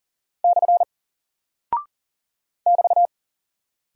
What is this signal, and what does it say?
Morse code, heard as a keyed tone, about 40 wpm, 700 Hz. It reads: / <BT>
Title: Single Character (Letter:Number:Punctuation:Prosign) Alt 40wpm